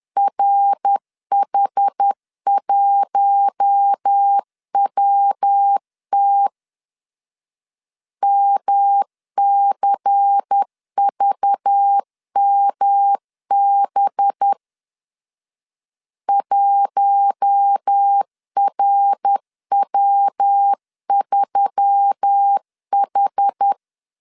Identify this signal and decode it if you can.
Morse code, heard as a keyed tone, about 10 wpm, 800 Hz. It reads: RH1WT MCVMB 1RW3H
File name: LEZIONI / 6 PKX0 / 6 PKX 10wpm 2426